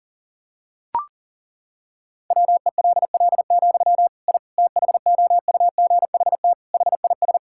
Morse code, heard as a keyed tone, about 40 wpm, 700 Hz. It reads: WELL, I THOUGHT HIS
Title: Sentences from Top 1000 Words - Review Part 5 - 40wpm